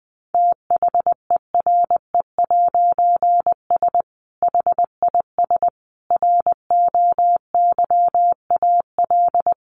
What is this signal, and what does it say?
Morse code, heard as a keyed tone, about 20 wpm, 700 Hz. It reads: THERE'S HIS ROYAL
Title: A Sound Of Thunder 20wpm